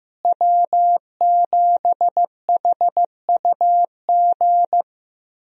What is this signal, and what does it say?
Morse code, heard as a keyed tone, about 15 wpm, 700 Hz. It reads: W7HUG
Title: USA Call Signs - 5 Characters - Repeated 2x - 15wpm